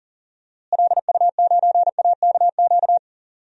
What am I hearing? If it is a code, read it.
Morse code, heard as a keyed tone, about 40 wpm, 700 Hz. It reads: LU9AKQ
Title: DXCC-100 Call Signs - 6 Characters - Repeated 3x - 40wpm